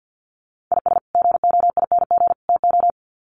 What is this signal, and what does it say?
Morse code, heard as a keyed tone, about 50 wpm, 700 Hz. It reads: IS GOING TO
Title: Sets of 3 Words 50wpm